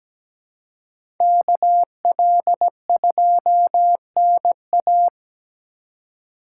Morse code, heard as a keyed tone, about 17 wpm, 700 Hz. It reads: KL2NA
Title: USA Call Signs - 5 Characters - 2x Character Spacing - 17wpm